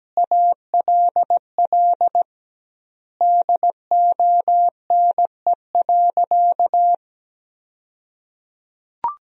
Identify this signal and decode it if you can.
Morse code, heard as a keyed tone, about 17 wpm, 700 Hz. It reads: ALL DONE.
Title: 2x Word-Spacing - Sentences from Top 500 Words - 17wpm